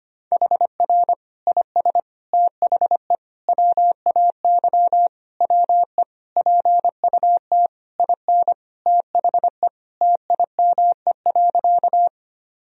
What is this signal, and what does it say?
Morse code, heard as a keyed tone, about 25 wpm, 700 Hz. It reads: HR IS THE WAY WE PUT IN THE TIME.
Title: Sentences from Top 200 Words and Abbreviations 25wpm